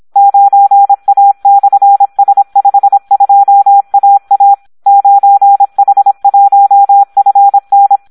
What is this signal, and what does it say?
Morse code, heard as a keyed tone, about 26 wpm, 800 Hz. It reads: 9A/S52AA 9H1FN
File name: Audio practica CW / SD 1 26